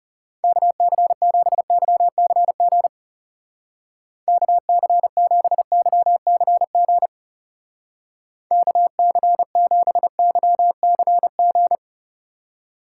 Morse code, heard as a keyed tone, about 37 wpm, 700 Hz. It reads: KC7YCG KC7YCG KC7YCG
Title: Speed Racing - USA 2x3 Call Signs 30wpm